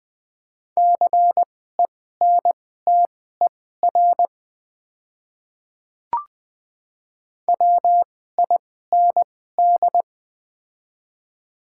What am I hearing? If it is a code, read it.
Morse code, heard as a keyed tone, about 20 wpm, spaced as for 14 wpm, 690 Hz. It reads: CENTER WIND
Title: 2x Character Spacing - Top 500 Words - 20wpm